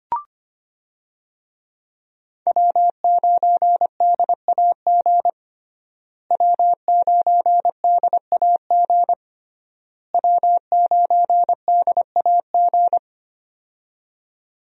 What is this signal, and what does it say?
Morse code, heard as a keyed tone, about 25 wpm, 700 Hz. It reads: W9DAG W9DAG W9DAG
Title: USA Call Signs - 5 Characters - Repeated 3x - 25wpm